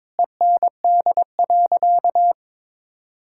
Morse code, heard as a keyed tone, about 22 wpm, 700 Hz. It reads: END.
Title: Sentences from Top 1000 Words - Review Part 5 - 22wpm